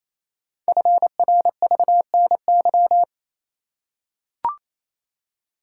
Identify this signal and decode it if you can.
Morse code, heard as a keyed tone, about 29 wpm, 700 Hz. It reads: FRVNY
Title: Code Groups - 5 Letters - 28wpm